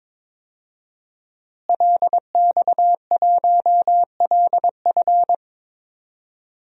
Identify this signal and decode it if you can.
Morse code, heard as a keyed tone, about 22 wpm, 700 Hz. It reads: LX1LF
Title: DXCC-100 Call Signs - 5 Characters - Repeated 3x - 22wpm